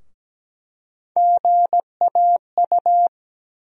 Morse code, heard as a keyed tone, about 17 wpm, 700 Hz. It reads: GAU
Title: USA Call Sign Postfixes - 3 Letters 17wpm